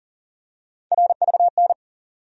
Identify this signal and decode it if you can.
Morse code, heard as a keyed tone, about 40 wpm, 700 Hz. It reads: RVN
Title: USA Call Sign Postfixes - 3 Letters 40wpm